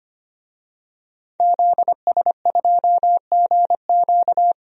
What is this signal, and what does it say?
Morse code, heard as a keyed tone, about 25 wpm, 700 Hz. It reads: ZS2GQ
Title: DXCC-100 Call Signs - 5 Characters - 25wpm